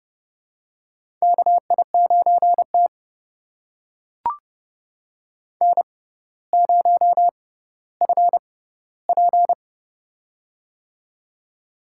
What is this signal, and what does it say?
Morse code, heard as a keyed tone, about 30 wpm, spaced as for 15 wpm, 700 Hz. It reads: KI9T N0FP
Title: USA Call Signs - 4 Characters - 6x Character Spacing - 30wpm